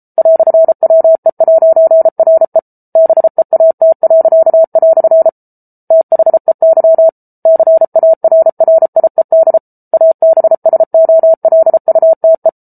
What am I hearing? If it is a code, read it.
Morse code, heard as a keyed tone, about 34 wpm, 650 Hz. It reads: "WE'RE BEAT." THEY CARRIED ABSOLUTE